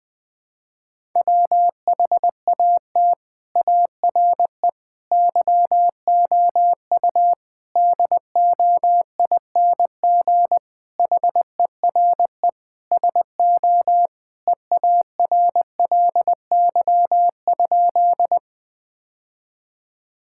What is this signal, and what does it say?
Morse code, heard as a keyed tone, about 21 wpm, 700 Hz. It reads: WHAT ARE YOU DOING HERE SO EARLY?
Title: Sentences with 5-Word Phrases - 20wpm